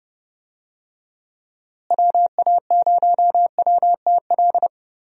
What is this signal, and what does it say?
Morse code, heard as a keyed tone, about 31 wpm, 700 Hz. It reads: WA0WTL
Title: USA Call Signs - 6 Characters - Repeated 2x - 30wpm